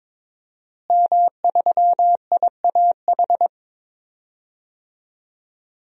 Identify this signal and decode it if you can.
Morse code, heard as a keyed tone, about 22 wpm, 700 Hz. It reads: M3IAH
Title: DXCC-100 Call Signs - 5 Characters - Repeated 2x - 22wpm